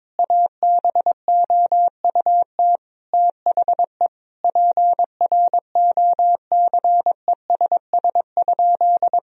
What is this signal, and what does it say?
Morse code, heard as a keyed tone, about 22 wpm, 700 Hz. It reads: ABOUT THE PROCESS?
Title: Sentences from Top 1000 Words - Review Part 3 - 22wpm